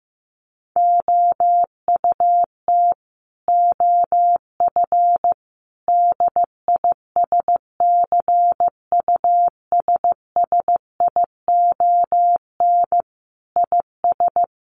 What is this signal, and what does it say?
Morse code, heard as a keyed tone, about 15 wpm, 700 Hz. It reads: OUT OF DISCUSSION IS
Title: Sentences from Top 1100 Words 15wpm